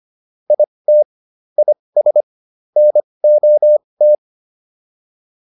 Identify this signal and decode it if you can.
Morse code, heard as a keyed tone, about 25 wpm, spaced as for 20 wpm, 600 Hz. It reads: IT IS NOT